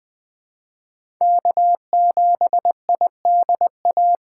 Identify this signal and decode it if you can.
Morse code, heard as a keyed tone, about 20 wpm, 700 Hz. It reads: K7IDA
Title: USA Call Signs - 3x Character Spacing - 20wpm